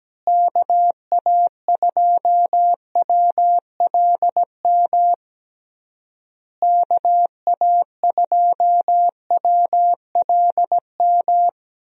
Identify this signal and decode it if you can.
Morse code, heard as a keyed tone, about 17 wpm, 700 Hz. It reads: KA2WLM KA2WLM
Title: USA Call Signs - 6 Characters - Repeated 3x - 17wpm